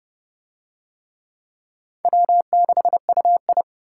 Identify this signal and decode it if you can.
Morse code, heard as a keyed tone, about 30 wpm, 700 Hz. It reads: W6UI